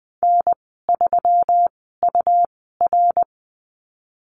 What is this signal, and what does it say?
Morse code, heard as a keyed tone, about 20 wpm, spaced as for 15 wpm, 700 Hz. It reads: N3UR
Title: 140 Lesson 23g - Letter f - Call Signs 20wpm